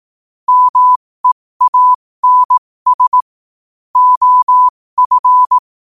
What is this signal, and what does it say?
Morse code, heard as a keyed tone, about 18 wpm, spaced as for 15 wpm, 1000 Hz. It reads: MEANS OF
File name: qotdfiles / 15wpm